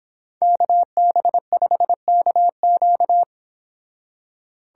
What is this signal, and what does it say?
Morse code, heard as a keyed tone, about 31 wpm, 700 Hz. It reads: KB5KQ